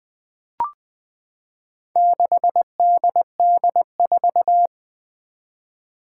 Morse code, heard as a keyed tone, about 20 wpm, 700 Hz. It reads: 6DD4
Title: Four Challenging Letter-Number Combinations 20wpm